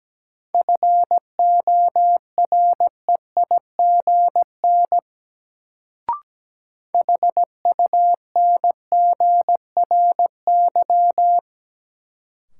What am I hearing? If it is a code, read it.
Morse code, heard as a keyed tone, about 17 wpm, 700 Hz. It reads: FOREIGN HUNGRY
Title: Sentences from Top 1400 Words - New Word Drill - 17wpm